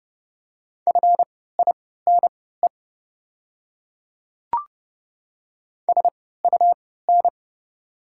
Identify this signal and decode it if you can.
Morse code, heard as a keyed tone, about 30 wpm, spaced as for 17 wpm, 690 Hz. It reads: FINE SUN
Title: 3x Character Spacing - Top 500 Words - 30wpm